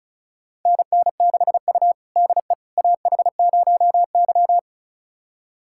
Morse code, heard as a keyed tone, about 36 wpm, 700 Hz. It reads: NN6U DE AH0Y
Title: USA Callsign DE Callsign - 4 Characters - 35wpm